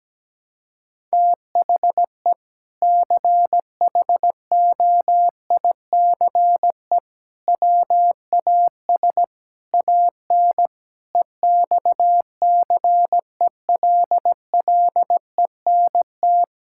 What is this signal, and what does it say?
Morse code, heard as a keyed tone, about 17 wpm, 700 Hz. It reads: THE CHOICE WAS AN EXCELLENT
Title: Sentences from Top 1000 Words - Review Part 6 - 17wpm